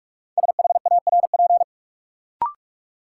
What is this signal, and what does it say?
Morse code, heard as a keyed tone, about 46 wpm, 700 Hz. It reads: SHARP
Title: Sentences from Top 1600 Words - New Word Drill - 45wpm